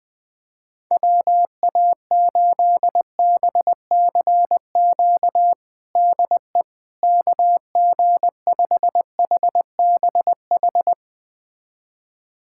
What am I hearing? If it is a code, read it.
Morse code, heard as a keyed tone, about 20 wpm, 700 Hz. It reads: WA8BCQ DE KG5HBH
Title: USA Callsign DE Callsign - 6 Characters - 20wpm